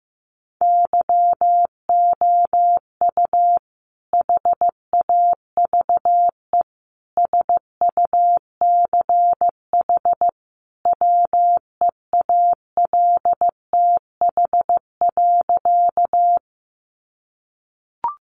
Morse code, heard as a keyed tone, about 15 wpm, 700 Hz. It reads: YOU HAVE SUCH WEALTH.
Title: Sentences from Top 1200 Words 15wpm